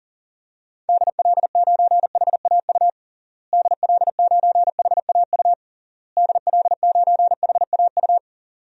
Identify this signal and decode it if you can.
Morse code, heard as a keyed tone, about 40 wpm, 700 Hz. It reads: DL9HAU DL9HAU DL9HAU
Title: DXCC-100 Call Signs - 6 Characters - Repeated 3x - 40wpm